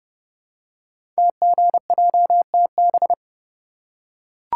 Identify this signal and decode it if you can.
Morse code, heard as a keyed tone, about 30 wpm, 690 Hz. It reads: TGJTB
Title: Code Groups - 5 Letters - 30wpm